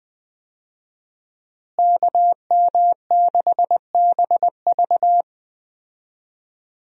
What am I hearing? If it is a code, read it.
Morse code, heard as a keyed tone, about 20 wpm, 700 Hz. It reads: KM6BV
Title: USA Call Signs - 5 Characters - Repeated 2x - 20wpm